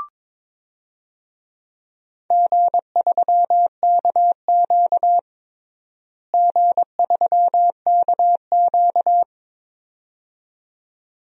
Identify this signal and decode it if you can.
Morse code, heard as a keyed tone, about 22 wpm, 700 Hz. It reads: G3KQ G3KQ
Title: DXCC-100 Call Signs - 4 Characters - Repeated 2x - 22wpm